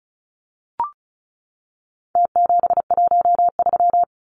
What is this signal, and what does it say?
Morse code, heard as a keyed tone, about 34 wpm, 700 Hz. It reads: T713
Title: Numbers - Contest Style - 4-Digit 35wpm